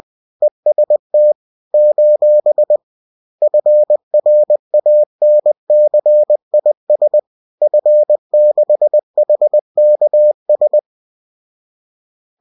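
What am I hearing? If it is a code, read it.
Morse code, heard as a keyed tone, about 20 wpm, 600 Hz. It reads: EST : FRANCIS F6HKS
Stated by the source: Bulletin sent in CW.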